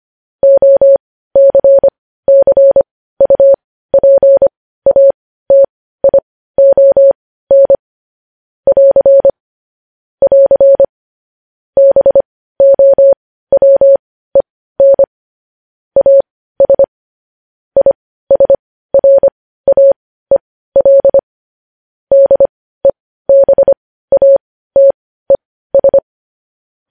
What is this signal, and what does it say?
Morse code, heard as a keyed tone, about 25 wpm, spaced as for 15 wpm, 550 Hz. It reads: OCCUPATION <AR> <AR> BOWEN AS ISRAEL DEBATES
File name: podcast / News headlines-1x-25x15